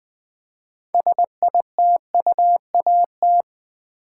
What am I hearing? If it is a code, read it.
Morse code, heard as a keyed tone, about 20 wpm, 700 Hz. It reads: SITUAT
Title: Build A Word - 3 Letters At A Time - 9-Letter Words - 20wpm